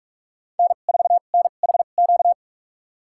Morse code, heard as a keyed tone, about 46 wpm, spaced as for 32 wpm, 700 Hz. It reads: N4NHQ
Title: USA Call Signs - 2x Character Spacing - 45wpm